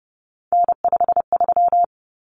Morse code, heard as a keyed tone, about 30 wpm, 700 Hz. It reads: N53
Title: Common Cut Numbers - 3-Digit 30wpm